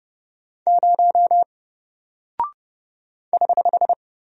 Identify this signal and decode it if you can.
Morse code, heard as a keyed tone, about 30 wpm, 700 Hz. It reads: 0 <HH>
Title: Single Character (Letter:Number:Punctuation:Prosign) Alt 30wpm